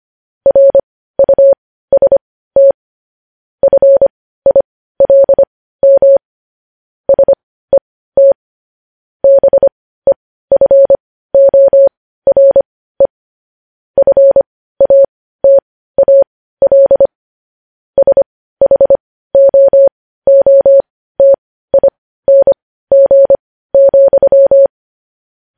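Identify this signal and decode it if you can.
Morse code, heard as a keyed tone, about 25 wpm, spaced as for 15 wpm, 550 Hz. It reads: RUST FILM SET BEFORE FATAL SHOOTING,
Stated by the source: Morse code news-headline practice sent at 15 words per minute.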